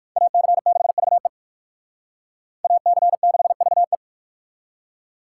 Morse code, heard as a keyed tone, about 56 wpm, 700 Hz. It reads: AC6VE AC6VE
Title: Speed Racing - USA 2x2 Call Signs 45wpm